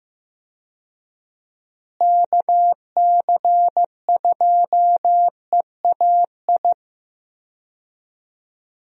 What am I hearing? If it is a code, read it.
Morse code, heard as a keyed tone, about 15 wpm, 700 Hz. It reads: KC2EAI
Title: USA Call Signs - 6 Characters - Repeated 2x - 15wpm